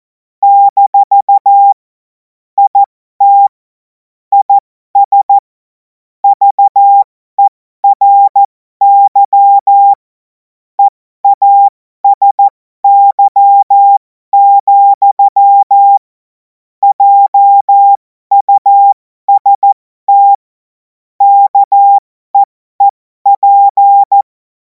Morse code, heard as a keyed tone, about 14 wpm, spaced as for 12 wpm, 800 Hz. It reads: - IT IS VERY EASY, JUST KEEP
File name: Morsekurs / Witze